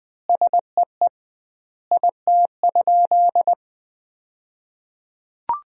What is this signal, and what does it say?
Morse code, heard as a keyed tone, about 21 wpm, 700 Hz. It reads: SEE IT?
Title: 2x Word-Spacing - Sentences from Top 100 Words - 20wpm